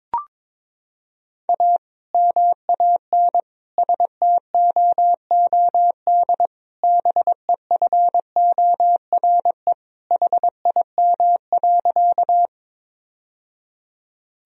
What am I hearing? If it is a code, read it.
Morse code, heard as a keyed tone, about 22 wpm, 700 Hz. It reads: A MAN STOOD BEFORE HIM.
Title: Sentences from Top 1000 Words - Review Part 2 - 22wpm